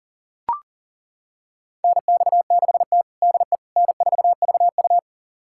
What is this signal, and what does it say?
Morse code, heard as a keyed tone, about 40 wpm, 700 Hz. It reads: NX6T DE N4VU
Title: USA Callsign DE Callsign - 4 Characters - 40wpm